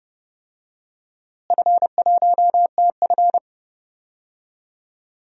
Morse code, heard as a keyed tone, about 30 wpm, 700 Hz. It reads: F1TF
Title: DXCC-100 Call Signs - 4 Characters - Repeated 2x - 30wpm